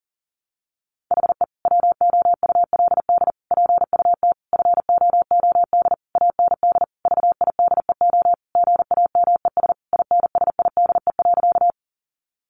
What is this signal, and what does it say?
Morse code, heard as a keyed tone, about 40 wpm, 700 Hz. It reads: HE WOULD PUT FOOD AND VIDEO GAMES INSIDE.
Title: Easy Phrases 40wpm